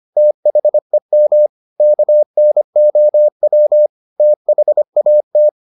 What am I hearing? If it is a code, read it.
Morse code, heard as a keyed tone, about 25 wpm, 600 Hz. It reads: THEM KNOW THAT